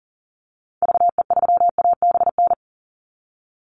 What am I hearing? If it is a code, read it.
Morse code, heard as a keyed tone, about 40 wpm, 700 Hz. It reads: VE3ABN
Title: Call Signs Repeated 2x - 40wpm